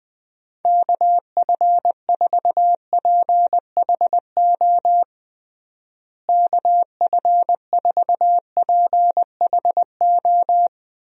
Morse code, heard as a keyed tone, about 20 wpm, 700 Hz. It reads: KF4PHO KF4PHO
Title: USA Call Signs - 6 Characters - Repeated 3x - 20wpm